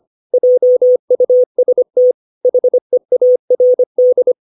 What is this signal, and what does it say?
Morse code, heard as a keyed tone, about 25 wpm, 500 Hz. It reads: JUST HEARD
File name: 632 words / 632 Words 25WPM